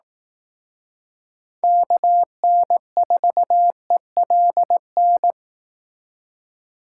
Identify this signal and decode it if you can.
Morse code, heard as a keyed tone, about 19 wpm, 700 Hz. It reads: KN4ELN
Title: Speed Racing - USA 2x3 Call Signs 15wpm